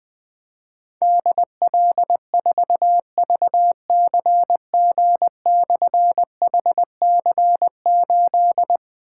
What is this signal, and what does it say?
Morse code, heard as a keyed tone, about 20 wpm, 700 Hz. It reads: DL4VCG/HC8
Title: Difficult Call Signs 20wpm